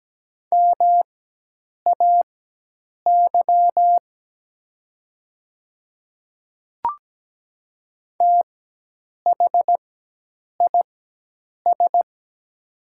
Morse code, heard as a keyed tone, about 17 wpm, spaced as for 8 wpm, 700 Hz. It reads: MAY THIS
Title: 4x Character Spacing - Top 100 Words - 17wpm